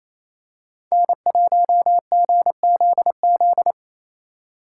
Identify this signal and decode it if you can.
Morse code, heard as a keyed tone, about 28 wpm, 700 Hz. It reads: N1GZZ
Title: USA Call Signs - 4x Character Spacing - 28wpm